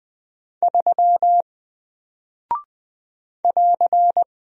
Morse code, heard as a keyed tone, about 20 wpm, 700 Hz. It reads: 3 <AR>
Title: Single Character (Letter:Number:Punctuation:Prosign) Alt 20wpm